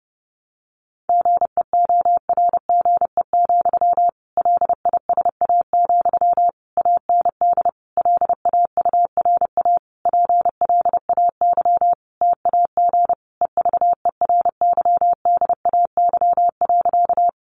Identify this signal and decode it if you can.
Morse code, heard as a keyed tone, about 30 wpm, 700 Hz. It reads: GEORGE, LISA, AND LAURA PLAY TAG EVERYDAY.
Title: Additional Easy Phrases 30wpm